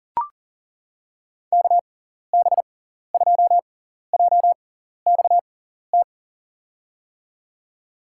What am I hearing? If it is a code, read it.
Morse code, heard as a keyed tone, about 40 wpm, spaced as for 14 wpm, 700 Hz. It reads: KB2JXT
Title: USA Call Signs - 6 Characters - 6x Character Spacing - 40wpm